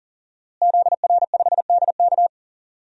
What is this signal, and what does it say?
Morse code, heard as a keyed tone, about 40 wpm, 700 Hz. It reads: ZR5DK
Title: DXCC-100 Call Signs - 5 Characters - Repeated 2x - 40wpm